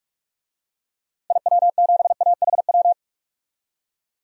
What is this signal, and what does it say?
Morse code, heard as a keyed tone, about 46 wpm, 700 Hz. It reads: IW7AHW
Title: DXCC-100 Call Signs - 6 Characters - 45wpm